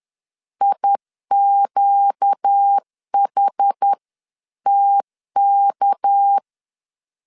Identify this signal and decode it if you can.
Morse code, heard as a keyed tone, about 10 wpm, 800 Hz. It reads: IQHTK
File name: LEZIONI / 6 PKX0 / 6 PK 10wpm 2436